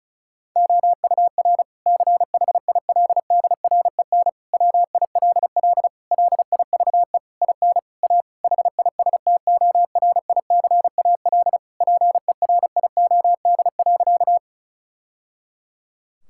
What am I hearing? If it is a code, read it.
Morse code, heard as a keyed tone, about 34 wpm, 700 Hz. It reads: OUR CHILDREN WILL LIVE IN A HISTORICAL PERIOD.
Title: Sentences from Top 1000 Words - Review Part 5 - 35wpm